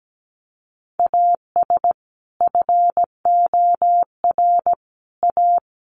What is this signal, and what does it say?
Morse code, heard as a keyed tone, about 17 wpm, 700 Hz. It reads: AS FOR A
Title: Sentences from Top 1400 Words 17wpm